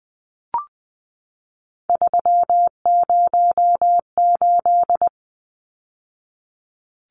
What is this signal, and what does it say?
Morse code, heard as a keyed tone, about 20 wpm, 700 Hz. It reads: 308
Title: Numbers - 3-Digit 20wpm